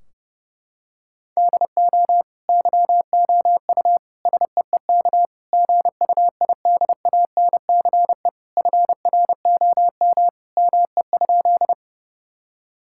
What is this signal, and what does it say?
Morse code, heard as a keyed tone, about 31 wpm, 700 Hz. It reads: DO YOU SEEK GUIDANCE FROM ME?
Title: Sentences from Top 1600 Words 30wpm